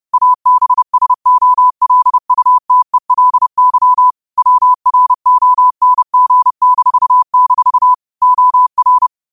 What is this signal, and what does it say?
Morse code, heard as a keyed tone, about 30 wpm, 1000 Hz. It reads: ABSOLUTELY WRONG-- OR
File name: qotdfiles / 30wpm